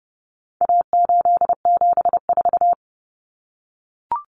Morse code, heard as a keyed tone, about 30 wpm, 700 Hz. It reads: A874
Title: Common Cut Numbers - 4-Digit 30wpm